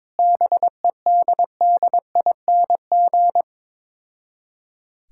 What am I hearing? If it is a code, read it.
Morse code, heard as a keyed tone, about 22 wpm, 690 Hz. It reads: BEDDING
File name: Difficult Words with Hints - 22wpm